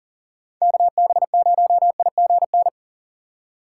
Morse code, heard as a keyed tone, about 41 wpm, 700 Hz. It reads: KB0IGN
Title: USA Call Signs - 6 Characters - 4x Character Spacing - 40wpm